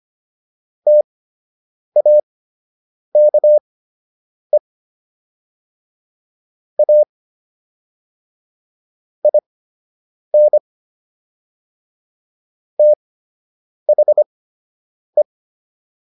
Morse code, heard as a keyed tone, about 25 wpm, spaced as for 8 wpm, 600 Hz. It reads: TAKE A IN THE